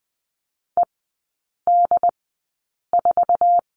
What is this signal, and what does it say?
Morse code, heard as a keyed tone, about 20 wpm, spaced as for 8 wpm, 700 Hz. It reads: ED4
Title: 074 Lesson 13e - Letter L - Three Characters 20wpm